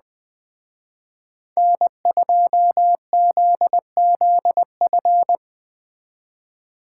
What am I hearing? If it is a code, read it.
Morse code, heard as a keyed tone, about 20 wpm, 700 Hz. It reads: N2ZZF
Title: Speed Racing - USA 1x3 Call Signs 17wpm